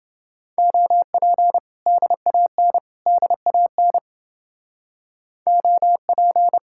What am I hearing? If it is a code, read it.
Morse code, heard as a keyed tone, about 29 wpm, 700 Hz. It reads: OP DAN DAN OP